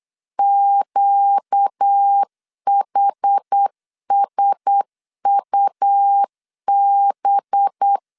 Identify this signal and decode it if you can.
Morse code, heard as a keyed tone, about 8 wpm, 800 Hz. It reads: QHSUB
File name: CW / DET / 1 6 1500C S A 8wpm 5203